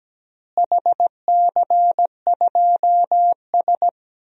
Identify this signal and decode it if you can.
Morse code, heard as a keyed tone, about 17 wpm, 700 Hz. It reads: HC2S
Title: DXCC-100 Call Signs - 4 Characters - Repeated 3x - 17wpm